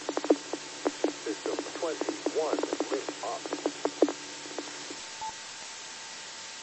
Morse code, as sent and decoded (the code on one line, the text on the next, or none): none